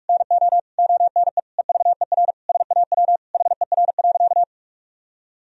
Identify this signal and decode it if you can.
Morse code, heard as a keyed tone, about 41 wpm, 700 Hz. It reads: NO ONE EVER SAW HER.
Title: Sentences from Top 1000 Words - Review Part 2 - 45wpm